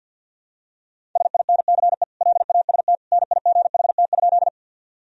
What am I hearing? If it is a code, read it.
Morse code, heard as a keyed tone, about 52 wpm, 700 Hz. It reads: SINCE LAST NIGHT?
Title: Sentences from Top 1000 Words - Review Part 1 - 50wpm